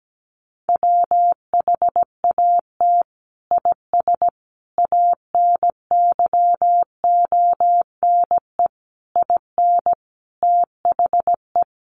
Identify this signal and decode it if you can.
Morse code, heard as a keyed tone, about 17 wpm, 700 Hz. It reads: WHAT IS ANYONE IN THE
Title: Sentences from Top 900 Words 17wpm